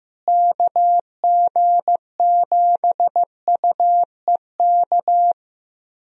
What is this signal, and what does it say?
Morse code, heard as a keyed tone, about 15 wpm, 700 Hz. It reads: KG7UEK
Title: USA Call Signs - 6 Characters - Repeated 3x - 15wpm